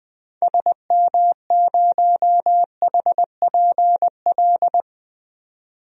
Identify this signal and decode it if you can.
Morse code, heard as a keyed tone, about 21 wpm, 700 Hz. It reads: SM0HPL
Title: DXCC-100 Call Signs - 6 Characters - 20wpm